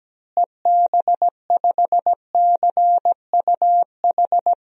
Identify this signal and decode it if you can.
Morse code, heard as a keyed tone, about 17 wpm, 700 Hz. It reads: EB5CUH
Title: DXCC-100 Call Signs - 6 Characters - Repeated 3x - 17wpm